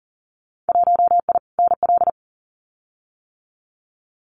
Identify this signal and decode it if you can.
Morse code, heard as a keyed tone, about 41 wpm, 700 Hz. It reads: 1I NL
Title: Winter Field Day Exchanges 40wpm